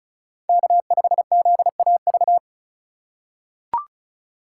Morse code, heard as a keyed tone, about 36 wpm, 700 Hz. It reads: K5ZAV
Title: USA Call Signs - 5 Characters - 2x Character Spacing - 35wpm